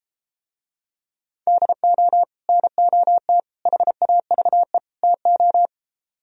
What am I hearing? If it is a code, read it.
Morse code, heard as a keyed tone, about 33 wpm, 700 Hz. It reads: DO NOT HAVE TO